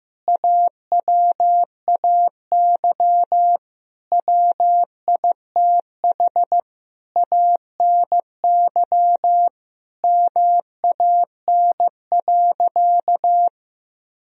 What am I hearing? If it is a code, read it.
Morse code, heard as a keyed tone, about 16 wpm, 700 Hz. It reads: AWAY WITH ANY MAN.
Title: Sentences from Top 1000 Words - Review Part 1 - 15wpm